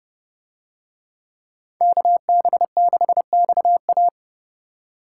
Morse code, heard as a keyed tone, about 30 wpm, 700 Hz. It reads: KB6XA